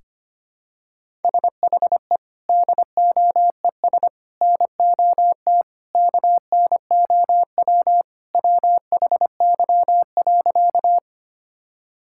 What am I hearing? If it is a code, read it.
Morse code, heard as a keyed tone, about 25 wpm, 700 Hz. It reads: SHE DOES NOT KNOW WHY.
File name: practice / 25wpm / Sentences with 4-Word Phrases - 25wpm